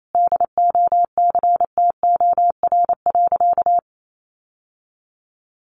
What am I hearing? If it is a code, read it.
Morse code, heard as a keyed tone, about 28 wpm, 700 Hz. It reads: DOCTOR.
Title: Sentences from Top 1500 Words 28wpm